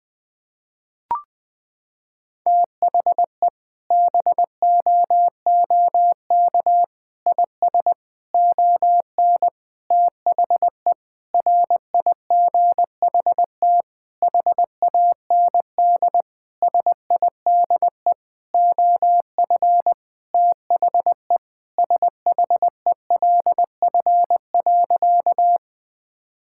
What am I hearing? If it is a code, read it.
Morse code, heard as a keyed tone, about 21 wpm, 700 Hz. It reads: THE BOOK IS ON THE RIGHT HAND SIDE OF THE SHELF.
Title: Sentences with 7-Word Phrases - 20wpm